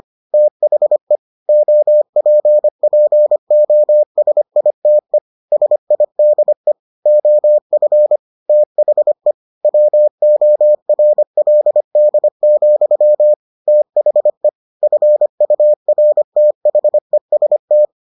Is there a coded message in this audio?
Yes: Morse code from 0.3 to 17.9 s, heard as a keyed tone, about 25 wpm, 600 Hz. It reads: THE OPPOSITE SIDE OF THE WORLD, THE FURTHEST